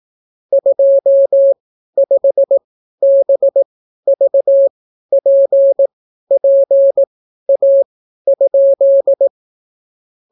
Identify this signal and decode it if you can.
Morse code, heard as a keyed tone, about 18 wpm, spaced as for 12 wpm, 550 Hz. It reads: 25BVPPA?